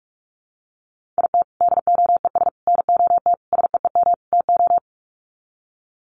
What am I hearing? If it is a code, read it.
Morse code, heard as a keyed tone, about 44 wpm, 700 Hz. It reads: IT DOES NOT SEEM TO
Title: Sets of 5 Words Encore2 45wpm